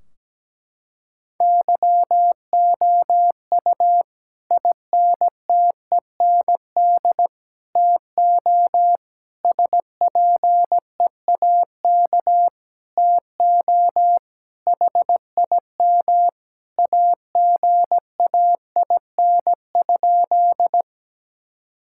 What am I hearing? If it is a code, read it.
Morse code, heard as a keyed tone, about 17 wpm, 700 Hz. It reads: YOU INTEND TO SPEAK TO HIM AGAIN?
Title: Sentences from Top 1600 Words 17wpm